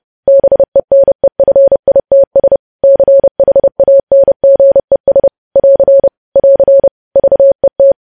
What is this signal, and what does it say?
Morse code, heard as a keyed tone, about 30 wpm, 550 Hz. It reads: BENEFITS CHANGES <AR> <AR> VET
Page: news headlines in morse code at 30 wpm